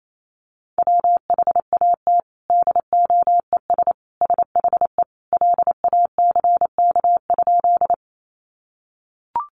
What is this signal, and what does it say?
Morse code, heard as a keyed tone, about 28 wpm, 700 Hz. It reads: WHAT DOES SHE LACK?
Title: Sentences from Top 1100 Words 28wpm